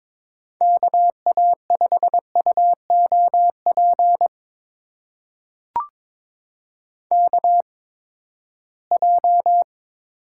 Morse code, heard as a keyed tone, about 22 wpm, spaced as for 15 wpm, 700 Hz. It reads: KA5UOP KJ
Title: USA Call Signs - 6 Characters - 8x Character Spacing - 22wpm